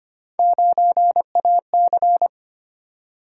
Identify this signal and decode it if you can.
Morse code, heard as a keyed tone, about 25 wpm, 700 Hz. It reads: 9AC
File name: Three Letter-Number Combinations - 25wpm